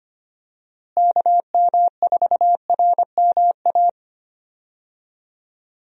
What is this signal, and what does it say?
Morse code, heard as a keyed tone, about 25 wpm, 700 Hz. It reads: KM4RMA
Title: USA Call Signs - 6 Characters - Repeated 2x - 25wpm